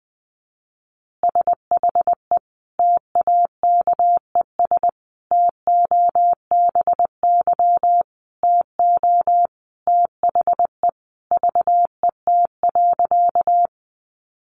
Morse code, heard as a keyed tone, about 20 wpm, 700 Hz. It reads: SHE TAKES TOBY TO THE VET.
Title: Easy Phrases 20wpm